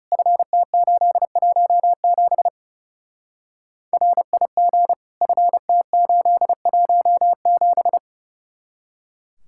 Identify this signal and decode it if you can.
Morse code, heard as a keyed tone, about 35 wpm, 700 Hz. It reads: FT817 RIG FT817
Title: Speed-Racing Common QSO Phrases - 30wpm